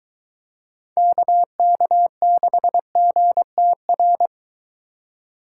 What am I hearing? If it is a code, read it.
Morse code, heard as a keyed tone, about 23 wpm, 700 Hz. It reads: KK6GTR